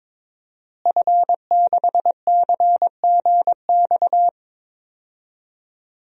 Morse code, heard as a keyed tone, about 22 wpm, 700 Hz. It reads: F6CGX
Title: DXCC-100 Call Signs - 5 Characters - Repeated 3x - 22wpm